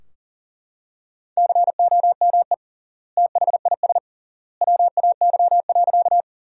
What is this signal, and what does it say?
Morse code, heard as a keyed tone, about 40 wpm, 700 Hz. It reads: COME THIS WAY.
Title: 3x Word-Spacing - Sentences from Top 100 Words - 40wpm